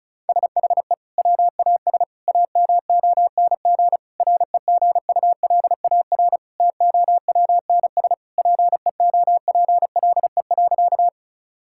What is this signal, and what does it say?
Morse code, heard as a keyed tone, about 36 wpm, 700 Hz. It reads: SHE WAS AMONG REGULAR TOWNS PEOPLE.
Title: Sentences from Top 1000 Words - Review Part 6 - 35wpm